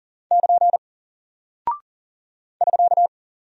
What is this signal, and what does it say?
Morse code, heard as a keyed tone, about 40 wpm, 700 Hz. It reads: <KN> <SK>
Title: Single Character (Letter:Number:Punctuation:Prosign) Alt 40wpm